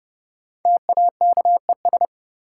Morse code, heard as a keyed tone, about 30 wpm, 700 Hz. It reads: TAKES
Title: Sets of 5 Words Encore - New Word Drill - 30wpm